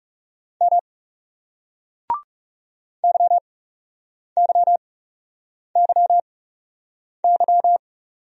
Morse code, heard as a keyed tone, about 39 wpm, spaced as for 8 wpm, 700 Hz. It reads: M YYYY